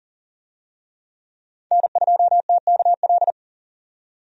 Morse code, heard as a keyed tone, about 40 wpm, 700 Hz. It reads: N2TKL
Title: USA Call Signs - 5 Characters - 40wpm